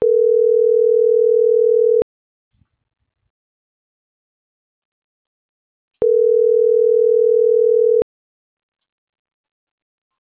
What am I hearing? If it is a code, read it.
Ringback tone (North America), 440 + 480 Hz.